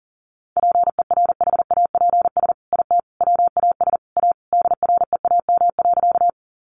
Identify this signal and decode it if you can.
Morse code, heard as a keyed tone, about 37 wpm, 700 Hz. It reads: PERHAPS IT WAS A DREAM.
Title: Sentences from Top 1500 Words 40wpm